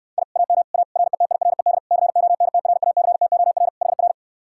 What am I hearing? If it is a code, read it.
Morse code, heard as a keyed tone, about 59 wpm, 700 Hz. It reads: I AM A RETIRED CONTRACTOR HR